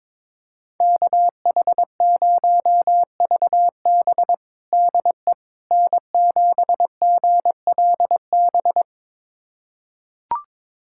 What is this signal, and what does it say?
Morse code, heard as a keyed tone, about 22 wpm, 700 Hz. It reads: KH0VB DE N7GLB
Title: USA Callsign DE Callsign - 5 Characters - 22wpm